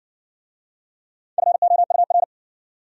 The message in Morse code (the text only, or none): FOUR